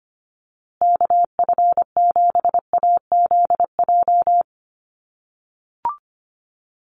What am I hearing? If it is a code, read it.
Morse code, heard as a keyed tone, about 25 wpm, 700 Hz. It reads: KF7AZJ
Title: USA Call Signs Repeated 3x 25wpm